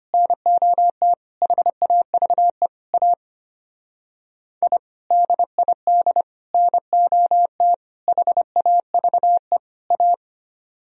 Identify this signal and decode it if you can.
Morse code, heard as a keyed tone, about 31 wpm, 700 Hz. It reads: NOT HAVE A I DID NOT HAVE A
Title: Speed-Racing - Sets of 5 Words 25wpm